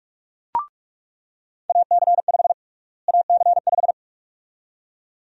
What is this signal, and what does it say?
Morse code, heard as a keyed tone, about 42 wpm, 700 Hz. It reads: AC5 AC5
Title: USA Call Sign Prefixes - Repeated 2x - 45wpm